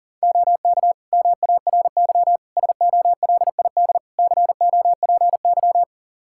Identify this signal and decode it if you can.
Morse code, heard as a keyed tone, about 40 wpm, 700 Hz. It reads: OK MARY SOLID COPY
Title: Practice QSO - Exchange 3 - More Info - 40wpm